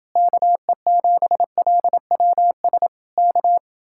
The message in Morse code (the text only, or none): KE7LWS K